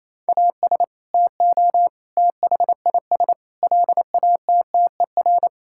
Morse code, heard as a keyed tone, about 28 wpm, 700 Hz. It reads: AS TO THIS LATTER
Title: Sentences from Top 1600 Words 28wpm